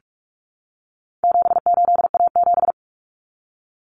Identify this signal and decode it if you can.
Morse code, heard as a keyed tone, about 44 wpm, 700 Hz. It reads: 78A7
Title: Common Cut Numbers - 4-Digit 45wpm